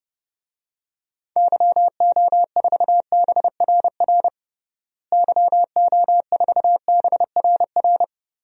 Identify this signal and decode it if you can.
Morse code, heard as a keyed tone, about 30 wpm, 700 Hz. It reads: YO4BRR YO4BRR
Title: DXCC-100 Call Signs - 6 Characters - Repeated 2x - 30wpm